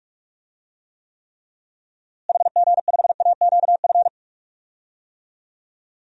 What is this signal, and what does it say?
Morse code, heard as a keyed tone, about 42 wpm, 700 Hz. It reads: HG5AQF
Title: DXCC-100 Call Signs - 6 Characters - Repeated 3x - 45wpm